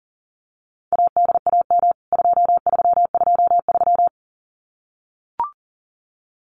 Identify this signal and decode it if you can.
Morse code, heard as a keyed tone, about 40 wpm, 700 Hz. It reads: ADAM 2323
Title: CWT Exchange 40wpm